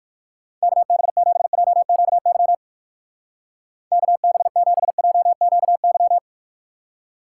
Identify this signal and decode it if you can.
Morse code, heard as a keyed tone, about 58 wpm, 700 Hz. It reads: KB7JQY KB7JQY